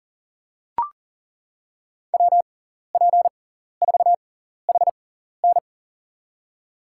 Morse code, heard as a keyed tone, about 37 wpm, spaced as for 14 wpm, 700 Hz. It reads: WP4HN
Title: USA Call Signs - 5 Characters - 6x Character Spacing - 40wpm